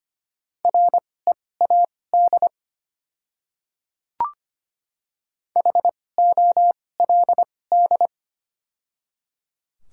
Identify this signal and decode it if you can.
Morse code, heard as a keyed tone, about 25 wpm, spaced as for 18 wpm, 700 Hz. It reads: READ HOLD
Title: 2x Character Spacing - Top 500 Words - 25wpm